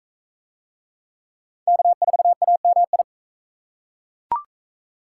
Morse code, heard as a keyed tone, about 35 wpm, 700 Hz. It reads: K4AMI